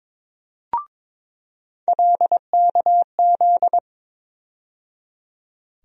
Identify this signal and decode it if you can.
Morse code, heard as a keyed tone, about 21 wpm, 700 Hz. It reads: LKZ
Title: USA Call Sign Postfixes - 3 Letters 22wpm